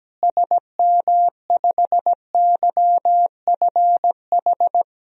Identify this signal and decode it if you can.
Morse code, heard as a keyed tone, about 17 wpm, 700 Hz. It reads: SM5YFH
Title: DXCC-100 Call Signs - 6 Characters - Repeated 2x - 17wpm